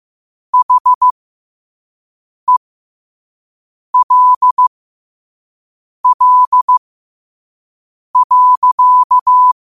Morse code, heard as a keyed tone, about 15 wpm, spaced as for 5 wpm, 1000 Hz. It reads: HELL.
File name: qotdfiles / 5wpm